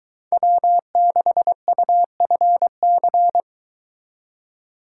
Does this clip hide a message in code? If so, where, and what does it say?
Yes: Morse code from 0.3 to 3.4 s, heard as a keyed tone, about 22 wpm, 700 Hz. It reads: W6UFC